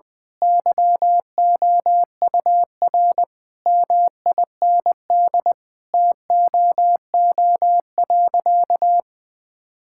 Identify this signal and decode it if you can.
Morse code, heard as a keyed tone, about 20 wpm, 700 Hz. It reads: YOUR MIND TOO.
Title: Sentences from Top 1000 Words - Review Part 6 - 20wpm